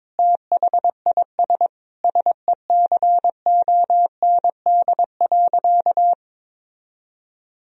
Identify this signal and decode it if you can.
Morse code, heard as a keyed tone, about 23 wpm, 700 Hz. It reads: THIS SECOND.
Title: Sentences from Top 1000 Words - Review Part 1 - 22wpm